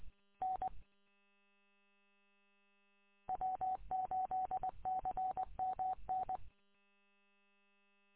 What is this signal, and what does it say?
Morse code, heard as a keyed tone, about 20 wpm, 750 Hz. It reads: N W8CMN
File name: w8cmn- -combined